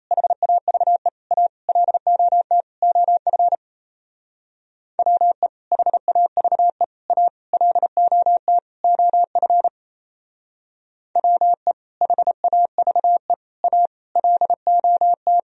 Morse code, heard as a keyed tone, about 35 wpm, 700 Hz. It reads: HAVE A LOT OF WE HAVE A LOT OF WE HAVE A LOT